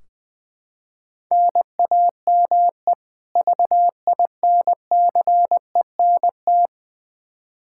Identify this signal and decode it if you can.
Morse code, heard as a keyed tone, about 20 wpm, 700 Hz. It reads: NAME VINCENT
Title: Practice QSO - Exchange 1 - Beginning - 20wpm